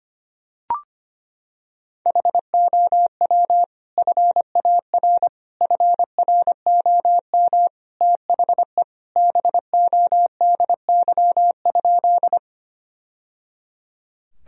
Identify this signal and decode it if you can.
Morse code, heard as a keyed tone, about 25 wpm, 700 Hz. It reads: HOW FAR FROM THE BODY?
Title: Sentences from Top 1000 Words - Review Part 2 - 25wpm